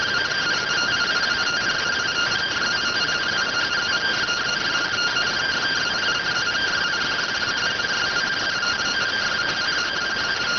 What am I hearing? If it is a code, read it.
Two-tone teleprinter signal, about 170 Hz shift.